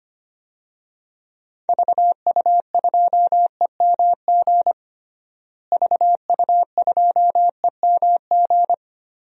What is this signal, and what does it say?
Morse code, heard as a keyed tone, about 25 wpm, 700 Hz. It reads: VU2EMG VU2EMG
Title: DXCC-100 Call Signs - 6 Characters - Repeated 2x - 25wpm